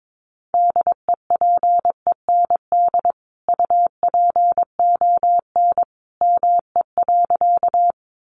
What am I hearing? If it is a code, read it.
Morse code, heard as a keyed tone, about 22 wpm, 700 Hz. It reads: DEPEND UPON ME.
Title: Sentences from Top 1300 Words 22wpm